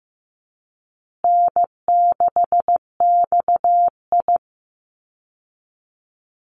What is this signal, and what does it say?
Morse code, heard as a keyed tone, about 15 wpm, 700 Hz. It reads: N6XI
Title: Easy Call Signs Repeated 3x - 15wpm